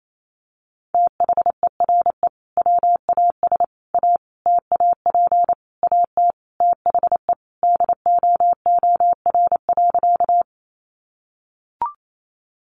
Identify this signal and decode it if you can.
Morse code, heard as a keyed tone, about 28 wpm, 700 Hz. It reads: THERE WAS A TAP AT THE DOOR.
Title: Sentences from Top 1400 Words 28wpm